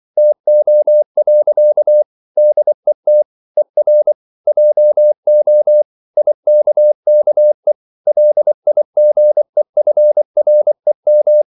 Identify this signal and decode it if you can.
Morse code, heard as a keyed tone, about 24 wpm, 600 Hz. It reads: TO. DET ER JO IKKE LIGEFREM
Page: Ferie 24 wpm